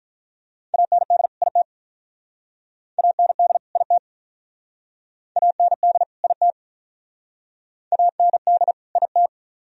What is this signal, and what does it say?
Morse code, heard as a keyed tone, about 46 wpm, 700 Hz. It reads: AND IT AND IT AND IT AND IT